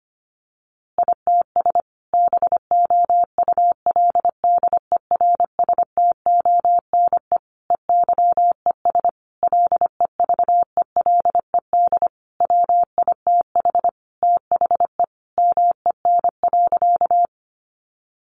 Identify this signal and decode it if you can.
Morse code, heard as a keyed tone, about 25 wpm, 700 Hz. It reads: ITS BOULDERSTONE EYES LEVELED WITH THE MEN.
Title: A Sound Of Thunder 25wpm